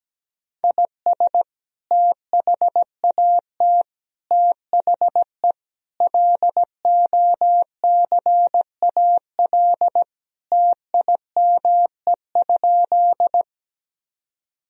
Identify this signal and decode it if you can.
Morse code, heard as a keyed tone, about 17 wpm, 700 Hz. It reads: IS THAT THE LOCAL TIME?
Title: Sentences from Top 1000 Words - Review Part 3 - 17wpm